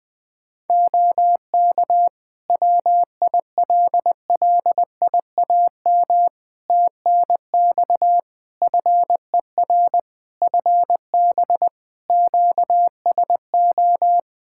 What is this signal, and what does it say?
Morse code, heard as a keyed tone, about 20 wpm, 700 Hz. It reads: OK WILLIAM TNX FER FB QSO
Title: Practice QSO - Exchange 4 - Ending - 20wpm